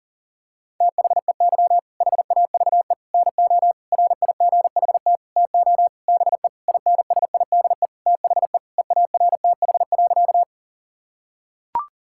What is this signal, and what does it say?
Morse code, heard as a keyed tone, about 38 wpm, 700 Hz. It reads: THEY HAVE NO RIGHT TO BE INSIDE THE EARTH.
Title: Sentences from Top 1000 Words - Review Part 6 - 40wpm